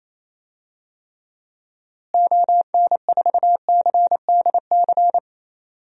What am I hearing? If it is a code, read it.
Morse code, heard as a keyed tone, about 28 wpm, 700 Hz. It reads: ON4CDC
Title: DXCC-100 Call Signs - 6 Characters - Repeated 3x - 28wpm